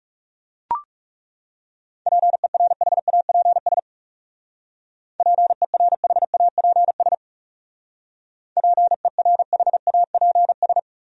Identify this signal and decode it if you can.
Morse code, heard as a keyed tone, about 38 wpm, 700 Hz. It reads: PERHAPS PERHAPS PERHAPS